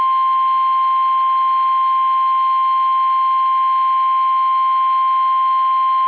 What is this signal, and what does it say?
Emergency Alert System (SAME) tones (the 1050 Hz attention tone). It is heard from 0.3 to 5.4 s.
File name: eas / KUSO / KUSO